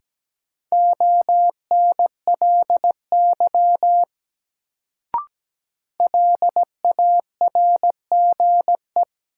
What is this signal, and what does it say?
Morse code, heard as a keyed tone, about 17 wpm, 700 Hz. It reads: ONLY LARGE
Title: Sentences from Top 200 Words - New Word Drill - 17wpm